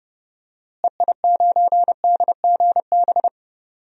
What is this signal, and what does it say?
Morse code, heard as a keyed tone, about 32 wpm, 700 Hz. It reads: EI9DGB
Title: DXCC-100 Call Signs - 6 Characters - Repeated 2x - 30wpm